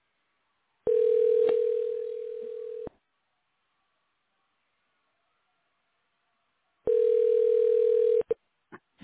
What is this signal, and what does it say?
Ringback tone (North America), 440 + 480 Hz.